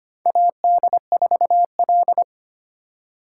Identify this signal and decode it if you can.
Morse code, heard as a keyed tone, about 25 wpm, 700 Hz. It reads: AD4L
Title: USA Call Signs - 4 Characters - 3x Character Spacing - 25wpm